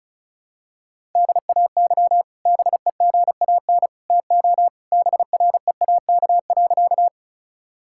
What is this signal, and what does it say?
Morse code, heard as a keyed tone, about 36 wpm, 700 Hz. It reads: DAY BEGAN TO BREAK.
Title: Sentences from Top 1000 Words - Review Part 6 - 35wpm